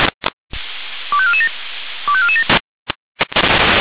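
Five-tone selective call (ZVEI), heard as quick runs of five tones, tones of about 80 ms. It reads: 25447 25447
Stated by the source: ZVEI five-tone selective call 25431 received by SDR.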